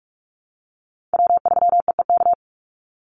Morse code, heard as a keyed tone, about 44 wpm, 700 Hz. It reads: W3EEK
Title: Call Signs Repeated 3x - 45wpm